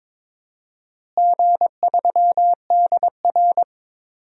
Morse code, heard as a keyed tone, about 22 wpm, 700 Hz. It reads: G3DR
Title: DXCC-100 Call Signs - 4 Characters - Repeated 3x - 22wpm